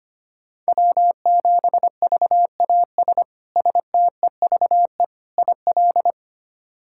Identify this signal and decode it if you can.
Morse code, heard as a keyed tone, about 25 wpm, 700 Hz. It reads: W7VAS STEVE IL
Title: K1USN SST QSO Practice - 25wpm